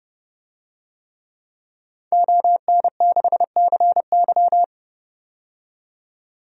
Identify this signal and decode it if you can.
Morse code, heard as a keyed tone, about 31 wpm, 700 Hz. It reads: ON6CY
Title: DXCC-100 Call Signs - 5 Characters - 30wpm